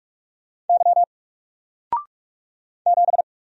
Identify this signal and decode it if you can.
Morse code, heard as a keyed tone, about 44 wpm, 700 Hz. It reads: Y 7
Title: Single Letter-Number 45wpm